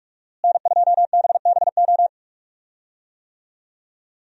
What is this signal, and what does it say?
Morse code, heard as a keyed tone, about 46 wpm, 700 Hz. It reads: N2BBO
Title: USA Call Signs - 5 Characters - Repeated 2x - 45wpm